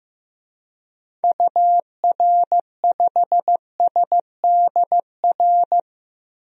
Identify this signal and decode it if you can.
Morse code, heard as a keyed tone, about 15 wpm, 700 Hz. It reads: UR5SDR
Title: DXCC-100 Call Signs - 6 Characters - Repeated 3x - 15wpm